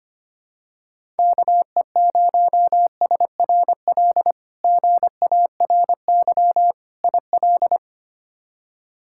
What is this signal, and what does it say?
Morse code, heard as a keyed tone, about 25 wpm, 700 Hz. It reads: KE0SRL GARY IL
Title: K1USN SST QSO Practice - 25wpm